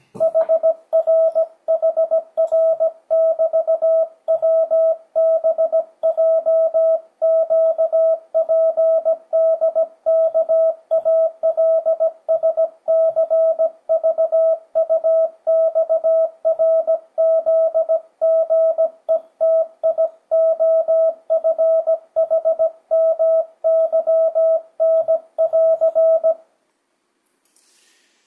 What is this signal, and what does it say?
Morse code, heard as a keyed tone, about 17 wpm, 650 Hz. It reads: HRHR<BT>WBJQPDKALSCVUXRZGETIOFHMYN<AR>
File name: 2025cw / speedup / 17wpm